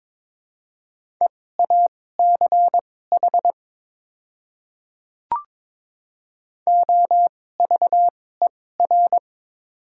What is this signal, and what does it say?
Morse code, heard as a keyed tone, about 22 wpm, spaced as for 16 wpm, 690 Hz. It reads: EACH OVER
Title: 2x Character Spacing - Top 100 Words - 22wpm